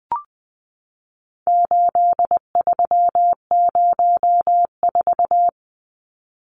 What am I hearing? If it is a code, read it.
Morse code, heard as a keyed tone, about 20 wpm, 700 Hz. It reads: 8304
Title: Numbers - 4-Digit 20wpm